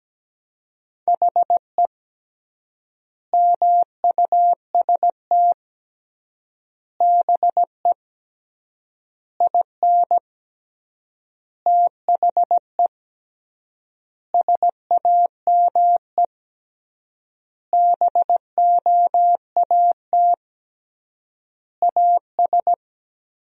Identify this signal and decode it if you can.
Morse code, heard as a keyed tone, about 17 wpm, 700 Hz. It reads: HE MUST BE IN THE SAME BOAT AS
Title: 3x Word-Spacing - Sentences from Top 500 Words - 17wpm